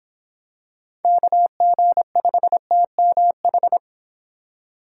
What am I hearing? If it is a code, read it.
Morse code, heard as a keyed tone, about 26 wpm, 700 Hz. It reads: KG5TMH